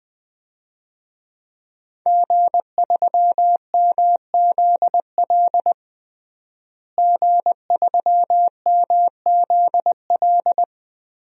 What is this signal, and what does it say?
Morse code, heard as a keyed tone, about 20 wpm, 700 Hz. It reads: G3MZL G3MZL
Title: DXCC-100 Call Signs - 5 Characters - Repeated 3x - 20wpm